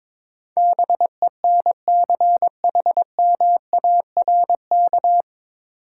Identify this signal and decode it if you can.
Morse code, heard as a keyed tone, about 22 wpm, 700 Hz. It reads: BENCHMARK
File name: 9 Letter Words - First 3 Letters Given - 22wpm